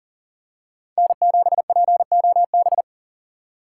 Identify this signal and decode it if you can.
Morse code, heard as a keyed tone, about 40 wpm, 700 Hz. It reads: N7POB
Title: USA Call Signs - 5 Characters - 3x Character Spacing - 40wpm